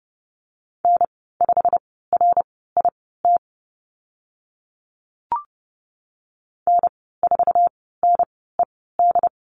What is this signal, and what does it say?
Morse code, heard as a keyed tone, about 31 wpm, spaced as for 17 wpm, 690 Hz. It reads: N5RIT N4NED
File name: Lesson 15g - Number 5 - Call Signs 30wpm